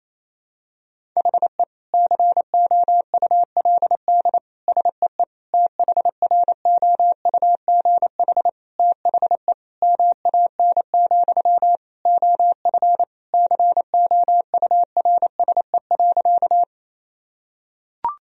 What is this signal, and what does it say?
Morse code, heard as a keyed tone, about 28 wpm, 700 Hz. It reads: HE COULD SEE THROUGH THE MAN, OF COURSE.
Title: Sentences from Top 1000 Words - Review Part 1 - 28wpm